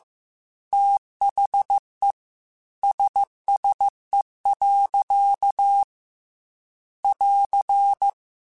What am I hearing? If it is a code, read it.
Morse code, heard as a keyed tone, about 15 wpm, 800 Hz. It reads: THE SSE. <AR>